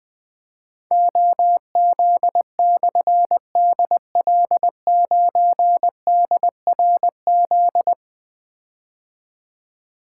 Morse code, heard as a keyed tone, about 20 wpm, 700 Hz. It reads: OZ/DL9DRZ
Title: Difficult Call Signs 20wpm